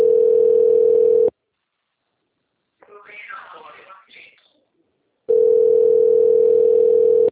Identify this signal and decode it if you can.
Ringback tone (North America), 440 + 480 Hz, with speech.
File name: out- - - - - .37632